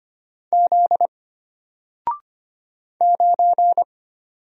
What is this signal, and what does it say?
Morse code, heard as a keyed tone, about 25 wpm, 700 Hz. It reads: Z 9
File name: Single-Letter-Number-25wpm